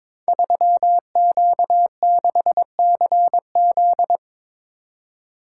Morse code, heard as a keyed tone, about 22 wpm, 700 Hz. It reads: 3Q6CZ
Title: Code Groups - 5 Letter:Numbers - 22wpm